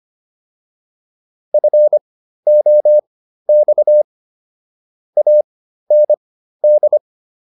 Morse code, heard as a keyed tone, about 25 wpm, spaced as for 13 wpm, 600 Hz. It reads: FOX AND